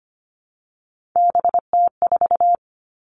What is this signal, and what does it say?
Morse code, heard as a keyed tone, about 25 wpm, 700 Hz. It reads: BT4
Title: All Cut Numbers - 3-Digit 25wpm